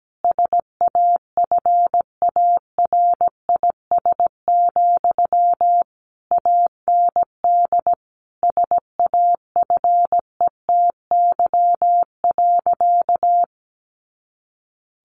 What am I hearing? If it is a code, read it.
Morse code, heard as a keyed tone, about 17 wpm, 700 Hz. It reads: SAFARIS, AND SAFETY.
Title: A Sound Of Thunder 17wpm